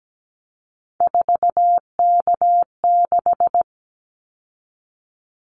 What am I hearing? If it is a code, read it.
Morse code, heard as a keyed tone, about 17 wpm, 700 Hz. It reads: 4K6
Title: Callsign Prefixes for Top 100 DXCC - 17wpm